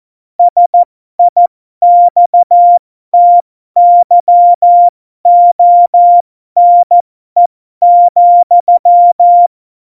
Morse code, heard as a keyed tone, about 14 wpm, spaced as for 12 wpm, 700 Hz. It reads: SIXTYONE,
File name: Morsekurs / Witze